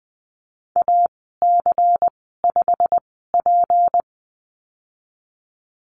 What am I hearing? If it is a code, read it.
Morse code, heard as a keyed tone, about 19 wpm, spaced as for 14 wpm, 700 Hz. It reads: AC5P
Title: 193 Lesson 31g - Letter v - Call Signs 20wpm